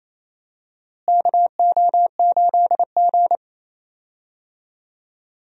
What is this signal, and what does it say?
Morse code, heard as a keyed tone, about 27 wpm, 700 Hz. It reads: KO8G
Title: USA Call Signs - 4 Characters - Repeated 2x - 28wpm